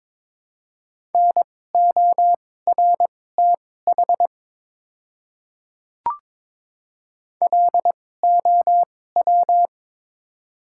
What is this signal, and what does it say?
Morse code, heard as a keyed tone, about 22 wpm, spaced as for 16 wpm, 700 Hz. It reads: NORTH LOW
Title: 2x Character Spacing - Top 300 Words - 22wpm